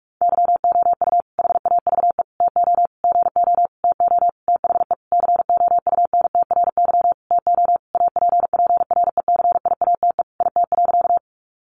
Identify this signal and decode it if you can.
Morse code, heard as a keyed tone, about 45 wpm, 700 Hz. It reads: YOU HAVE TO GO TO THE COUNTRY TO APPRECIATE IT.
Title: Sentences from Top 1300 Words 45wpm